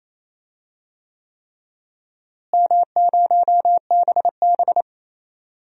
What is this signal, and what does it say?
Morse code, heard as a keyed tone, about 28 wpm, 700 Hz. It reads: M0BB
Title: DXCC-100 Call Signs - 4 Characters - Repeated 3x - 28wpm